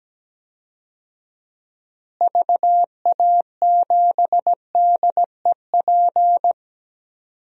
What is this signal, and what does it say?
Morse code, heard as a keyed tone, about 17 wpm, 700 Hz. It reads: VA7DEP
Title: DXCC-100 Call Signs - 6 Characters - Repeated 3x - 17wpm